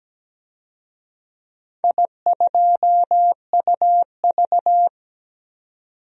Morse code, heard as a keyed tone, about 17 wpm, 700 Hz. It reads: I2UV
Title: DXCC-100 Call Signs - 4 Characters - 17wpm